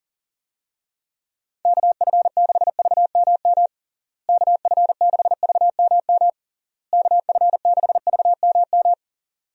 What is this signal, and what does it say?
Morse code, heard as a keyed tone, about 40 wpm, 700 Hz. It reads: KF6VMM KF6VMM KF6VMM
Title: USA Call Signs - 6 Characters - Repeated 3x - 40wpm